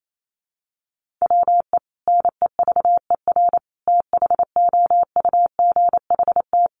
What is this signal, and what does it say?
Morse code, heard as a keyed tone, about 28 wpm, 700 Hz. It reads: WE NEVER THOUGHT
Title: Sentences from Top 300 Words 28wpm